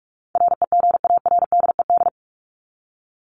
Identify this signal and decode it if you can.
Morse code, heard as a keyed tone, about 44 wpm, 700 Hz. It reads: REGARDED
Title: ED Suffix - Top 100 Words with Base Words 45wpm